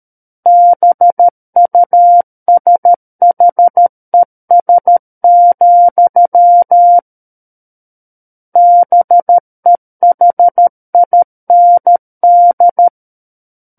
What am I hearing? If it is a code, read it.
Morse code, heard as a keyed tone, about 13 wpm, 700 Hz. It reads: BUSHES, BEHIND
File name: Book One / 04-Chapter 4 - 13 WPM